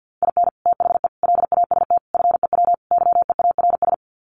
Morse code, heard as a keyed tone, about 50 wpm, 700 Hz. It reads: IN THE LAST FEW YEARS
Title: Sets of 5 Words 50wpm